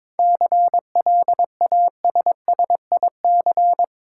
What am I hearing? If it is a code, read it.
Morse code, heard as a keyed tone, about 22 wpm, 700 Hz. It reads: CLASSIC
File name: Difficult Words with Hints - 22wpm